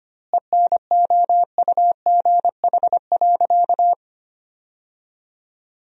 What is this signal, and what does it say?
Morse code, heard as a keyed tone, about 25 wpm, 700 Hz. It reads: ENOUGH.
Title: 2x Word-Spacing - Sentences from Top 400 Words - 25wpm